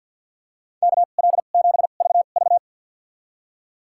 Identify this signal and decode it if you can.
Morse code, heard as a keyed tone, about 48 wpm, spaced as for 35 wpm, 700 Hz. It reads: KL7VV
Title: USA Call Signs - 5 Characters - 2x Character Spacing - 50wpm